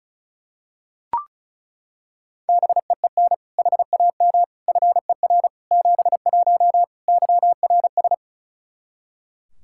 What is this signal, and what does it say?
Morse code, heard as a keyed tone, about 34 wpm, 700 Hz. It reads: BEEN HAM FER 71 YRS
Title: Practice QSO - Exchange 3 - More Info - 35wpm